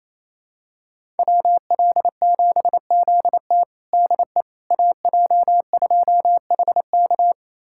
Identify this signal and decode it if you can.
Morse code, heard as a keyed tone, about 28 wpm, 700 Hz. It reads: WL7ZT DE AJ2HK
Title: USA Callsign DE Callsign - 5 Characters - 28wpm